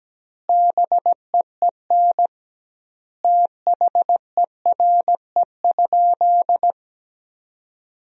Morse code, heard as a keyed tone, about 17 wpm, 700 Hz. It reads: BEEN THERE?
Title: 2x Word-Spacing - Sentences from Top 100 Words - 17wpm